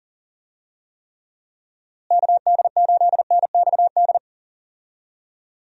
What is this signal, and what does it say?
Morse code, heard as a keyed tone, about 40 wpm, 700 Hz. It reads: KD8NXD
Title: USA Call Signs - 6 Characters - Repeated 3x - 40wpm